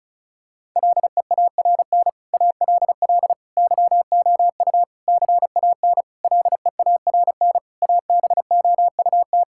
Morse code, heard as a keyed tone, about 36 wpm, 700 Hz. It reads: LEARN ALL YOU CAN LEARN ABOUT
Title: Sentences from Top 1000 Words - Review Part 5 - 35wpm